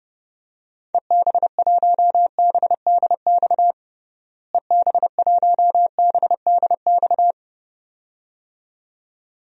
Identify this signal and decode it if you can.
Morse code, heard as a keyed tone, about 32 wpm, 700 Hz. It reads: EB1BDX EB1BDX
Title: DXCC-100 Call Signs - 6 Characters - Repeated 2x - 30wpm